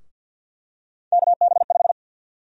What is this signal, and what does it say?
Morse code, heard as a keyed tone, about 50 wpm, 700 Hz. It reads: KB5
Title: USA Call Sign Prefixes - Repeated 2x - 50wpm